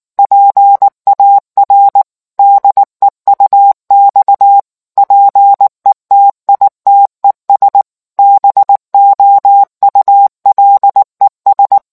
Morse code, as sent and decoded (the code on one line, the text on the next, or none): .--. .- .-. / -.. ..- -..- / .--. - .. - . ... / -... --- ..- .-.. ...
PAR DEUX PETITES BOULES